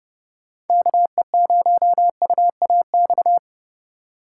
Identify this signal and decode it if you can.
Morse code, heard as a keyed tone, about 30 wpm, 700 Hz. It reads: KE0UAX